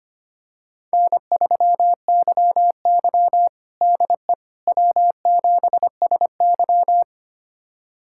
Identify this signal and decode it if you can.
Morse code, heard as a keyed tone, about 25 wpm, 700 Hz. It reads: N3YY DE W7SY
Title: USA Callsign DE Callsign - 4 Characters - 25wpm